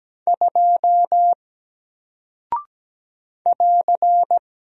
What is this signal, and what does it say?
Morse code, heard as a keyed tone, about 17 wpm, 700 Hz. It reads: 2 <AR>
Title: Single Character (Letter:Number:Punctuation:Prosign) 17wpm